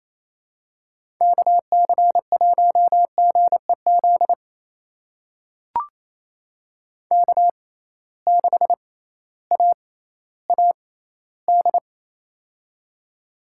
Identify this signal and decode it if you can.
Morse code, heard as a keyed tone, about 28 wpm, spaced as for 15 wpm, 700 Hz. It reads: KC1GEZ K6AAD
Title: USA Call Signs - 6x Character Spacing - 28wpm